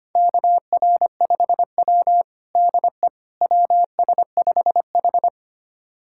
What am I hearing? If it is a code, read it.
Morse code, heard as a keyed tone, about 25 wpm, 700 Hz. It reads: KR5W DE WS5H
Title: USA Callsign DE Callsign - 4 Characters - 25wpm